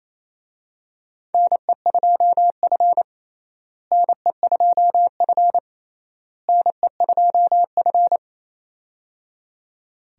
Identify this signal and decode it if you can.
Morse code, heard as a keyed tone, about 28 wpm, 700 Hz. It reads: NE2F NE2F NE2F
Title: USA Call Signs - 4 Characters - Repeated 3x - 28wpm